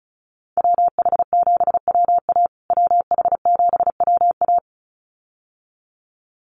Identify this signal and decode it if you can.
Morse code, heard as a keyed tone, about 34 wpm, 700 Hz. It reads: WH7WA WH7WA
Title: USA Call Signs Repeated 2x 35wpm